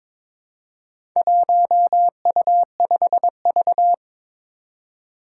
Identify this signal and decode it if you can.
Morse code, heard as a keyed tone, about 23 wpm, 700 Hz. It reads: 1U5V
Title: Four Challenging Letter-Number Combinations 22wpm